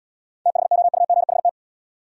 Morse code, heard as a keyed tone, about 75 wpm, 700 Hz. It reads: THOUGHT